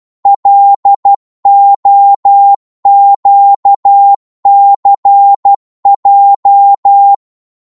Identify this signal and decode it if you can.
Morse code, heard as a keyed tone, about 12 wpm, 800 Hz. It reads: LOQCJ
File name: Morse / ltr-12wpm